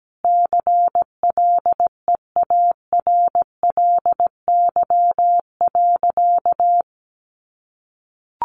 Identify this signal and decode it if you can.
Morse code, heard as a keyed tone, about 17 wpm, 700 Hz. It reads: CLEARLY.
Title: Sentences from Top 1300 Words 17wpm